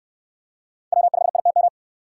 Morse code, heard as a keyed tone, about 69 wpm, 700 Hz. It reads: W5ITM